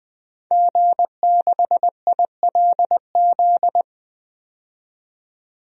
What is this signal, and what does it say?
Morse code, heard as a keyed tone, about 20 wpm, 700 Hz. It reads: G6ILZ
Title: DXCC-100 Call Signs - 5 Characters - 20wpm